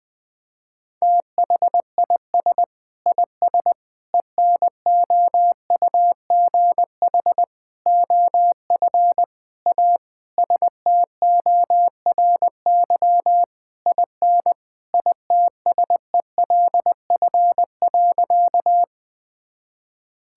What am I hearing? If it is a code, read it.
Morse code, heard as a keyed tone, about 20 wpm, 700 Hz. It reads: THIS IS ENOUGH OF A STORY IN ITSELF.
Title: Sentences from Top 1000 Words - Review Part 5 - 20wpm